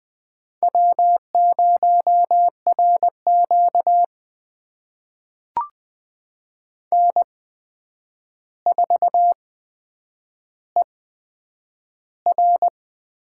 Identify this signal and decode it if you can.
Morse code, heard as a keyed tone, about 20 wpm, spaced as for 9 wpm, 700 Hz. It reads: W0RQ N4ER
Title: USA Call Signs - 4 Characters - 8x Character Spacing - 20wpm